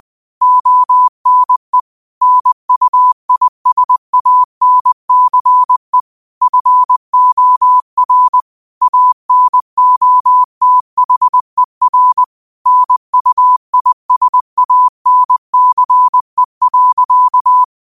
Morse code, heard as a keyed tone, about 20 wpm, 1000 Hz. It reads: ONE NUISANCE FOR ANOTHER NUISANCE.